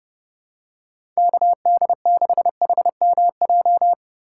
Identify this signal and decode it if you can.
Morse code, heard as a keyed tone, about 32 wpm, 700 Hz. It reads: KD6HMJ
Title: USA Call Signs - 6 Characters - 3x Character Spacing - 30wpm